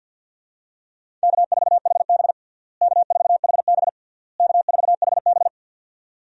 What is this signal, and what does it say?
Morse code, heard as a keyed tone, about 47 wpm, 700 Hz. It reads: K4HB K4HB K4HB
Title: USA Call Signs - 4 Characters - Repeated 3x - 50wpm